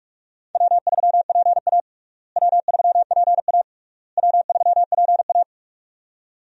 Morse code, heard as a keyed tone, about 41 wpm, 700 Hz. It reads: W3PA W3PA W3PA
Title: USA Call Signs - 4 Characters - Repeated 3x - 45wpm